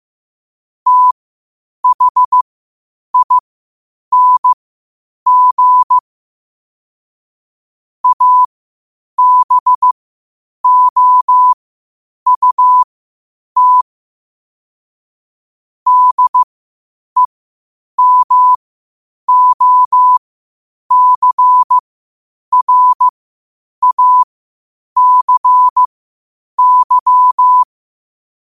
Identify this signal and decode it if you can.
Morse code, heard as a keyed tone, about 15 wpm, spaced as for 8 wpm, 1000 Hz. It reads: THING ABOUT DEMOCRACY
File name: qotdfiles / 8wpm